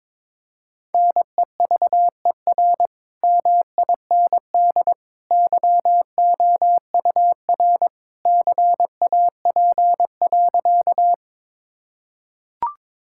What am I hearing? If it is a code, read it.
Morse code, heard as a keyed tone, about 22 wpm, 700 Hz. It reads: NEVER MIND YOUR CAP.
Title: Sentences from Top 1600 Words 22wpm